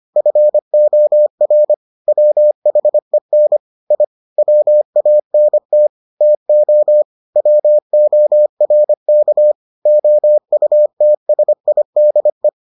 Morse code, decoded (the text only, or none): FOR WHEN I WANT TO WORK OUTSIDE